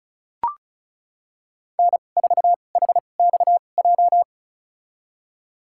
Morse code, heard as a keyed tone, about 36 wpm, spaced as for 25 wpm, 700 Hz. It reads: N4HXJ
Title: USA Call Signs - 5 Characters - 2x Character Spacing - 35wpm